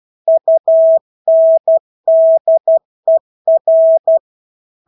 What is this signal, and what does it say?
Morse code, heard as a keyed tone, about 12 wpm, 650 Hz. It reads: UNDER
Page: Engelske ord 12 wpm